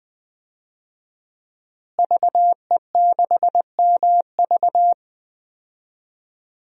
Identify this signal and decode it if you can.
Morse code, heard as a keyed tone, about 20 wpm, 700 Hz. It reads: VE6MV
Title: DXCC-100 Call Signs - 5 Characters - Repeated 3x - 20wpm